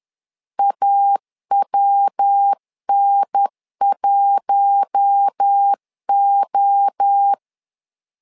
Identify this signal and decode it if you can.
Morse code, heard as a keyed tone, about 10 wpm, 800 Hz. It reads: AWN1O
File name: LEZIONI / 4 Q1 10wpm 2448